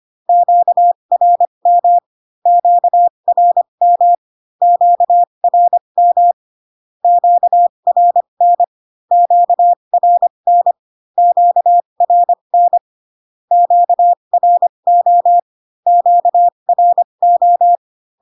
Morse code, heard as a keyed tone, about 25 wpm, spaced as for 22 wpm, 700 Hz. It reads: QRM QRM QRM QRN QRN QRN QRO QRO
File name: Q-codes most common repeated 3 times 18 25wpm